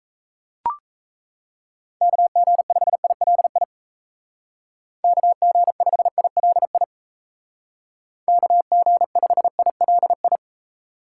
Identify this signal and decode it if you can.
Morse code, heard as a keyed tone, about 35 wpm, 700 Hz. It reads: KG5ILI KG5ILI KG5ILI
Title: Speed Racing - USA 2x3 Call Signs 28wpm